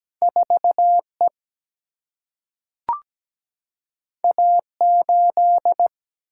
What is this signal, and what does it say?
Morse code, heard as a keyed tone, about 17 wpm, 700 Hz. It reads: 4E A8
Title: Single-Letter Single-Number 17wpm